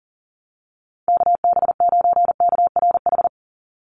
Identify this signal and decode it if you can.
Morse code, heard as a keyed tone, about 40 wpm, 700 Hz. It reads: KB9KRH
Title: USA Call Signs 40wpm